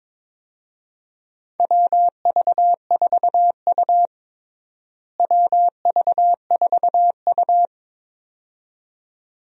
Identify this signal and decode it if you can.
Morse code, heard as a keyed tone, about 22 wpm, 700 Hz. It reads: WV4U WV4U
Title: USA Call Signs - 4 Characters - Repeated 2x - 22wpm